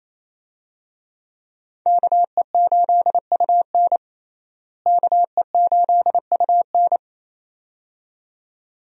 Morse code, heard as a keyed tone, about 28 wpm, 700 Hz. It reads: KE8UN KE8UN
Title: USA Call Signs - 5 Characters - Repeated 2x - 28wpm